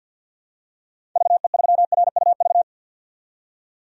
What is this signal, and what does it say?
Morse code, heard as a keyed tone, about 52 wpm, 700 Hz. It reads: VE3RUV
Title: DXCC-100 Call Signs - 6 Characters - 50wpm